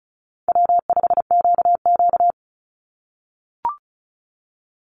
Morse code, heard as a keyed tone, about 34 wpm, 700 Hz. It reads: W5QQ
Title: Call Signs - 35wpm